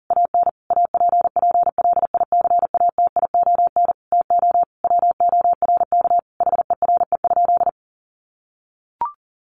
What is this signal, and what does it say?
Morse code, heard as a keyed tone, about 40 wpm, 700 Hz. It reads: AN APPLICATION TO WORK HERE?
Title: Sentences from Top 1200 Words 40wpm